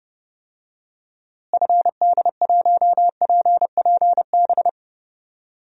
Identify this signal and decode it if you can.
Morse code, heard as a keyed tone, about 30 wpm, 700 Hz. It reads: FD1PPB
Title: DXCC-100 Call Signs - 6 Characters - Repeated 3x - 30wpm